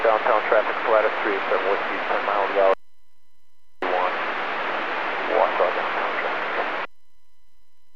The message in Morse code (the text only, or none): none